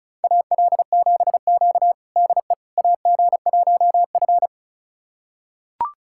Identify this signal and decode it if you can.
Morse code, heard as a keyed tone, about 36 wpm, 700 Hz. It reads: AL7Q DE AG1F
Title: USA Callsign DE Callsign - 4 Characters - 35wpm